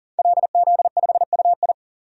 Morse code, heard as a keyed tone, about 40 wpm, 700 Hz. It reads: LZ5UI
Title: DXCC-100 Call Signs - 5 Characters - 40wpm